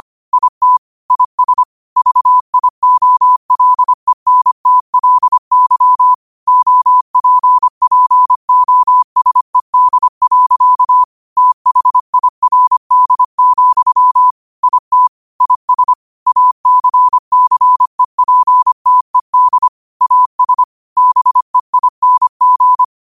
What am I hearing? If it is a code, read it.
Morse code, heard as a keyed tone, about 25 wpm, 1000 Hz. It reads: IT IS VIOLENTLY OPPOSED. THIRD, IT IS ACCEPTED AS BEING